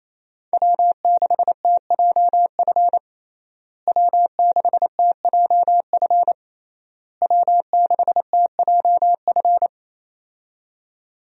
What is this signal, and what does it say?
Morse code, heard as a keyed tone, about 28 wpm, 700 Hz. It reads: W6TJF W6TJF W6TJF
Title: USA Call Signs - 5 Characters - Repeated 3x - 28wpm